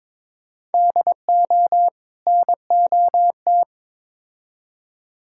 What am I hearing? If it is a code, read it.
Morse code, heard as a keyed tone, about 22 wpm, 700 Hz. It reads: DO NOT
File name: Build A Phrase - 5-Word Phrases - 22wpm